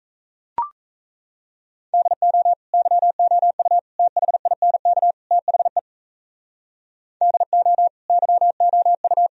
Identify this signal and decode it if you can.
Morse code, heard as a keyed tone, about 37 wpm, 700 Hz. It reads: DO YOU THINK THE DO YOU